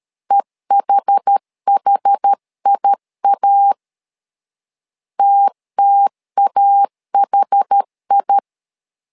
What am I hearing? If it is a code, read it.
Morse code, heard as a keyed tone, about 13 wpm, 800 Hz. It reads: EHHIA TTAHI
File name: LEZIONI / 2 UHDN / 2 UH 12wpm 1439